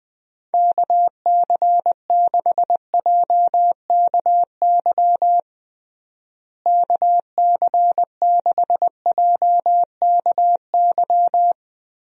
Morse code, heard as a keyed tone, about 21 wpm, 700 Hz. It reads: KC6JKY KC6JKY
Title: USA Call Signs - 6 Characters - Repeated 2x - 20wpm